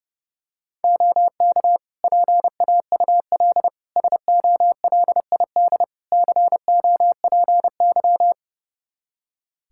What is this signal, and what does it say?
Morse code, heard as a keyed tone, about 30 wpm, 700 Hz. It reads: OK PAUL SOLID COPY
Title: K1USN SST QSO Practice 30wpm